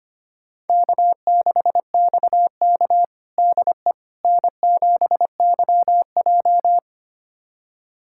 Morse code, heard as a keyed tone, about 25 wpm, 700 Hz. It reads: K6XK DE N7YJ
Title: USA Callsign DE Callsign - 4 Characters - 25wpm